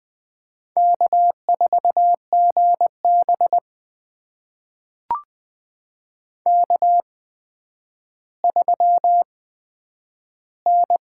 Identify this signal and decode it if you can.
Morse code, heard as a keyed tone, about 20 wpm, spaced as for 10 wpm, 700 Hz. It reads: K4GB K3N
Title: USA Call Signs - 8x Character Spacing - 20wpm